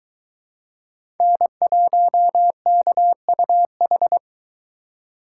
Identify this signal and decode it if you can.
Morse code, heard as a keyed tone, about 21 wpm, 700 Hz. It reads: N1KUH